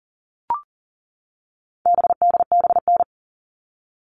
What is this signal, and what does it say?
Morse code, heard as a keyed tone, about 40 wpm, 700 Hz. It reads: BDBN
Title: All Cut Numbers - 4-Digit 40wpm